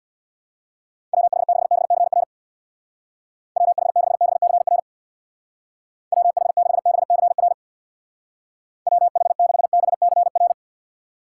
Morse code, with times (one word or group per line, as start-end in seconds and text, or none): 1.1-2.2 WH6BCR
3.6-4.8 WH6BCR
6.1-7.5 WH6BCR
8.9-10.5 WH6BCR